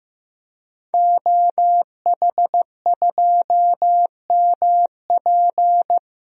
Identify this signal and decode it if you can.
Morse code, heard as a keyed tone, about 15 wpm, 700 Hz. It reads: OH2MP
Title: DXCC-100 Call Signs - 5 Characters - Repeated 2x - 15wpm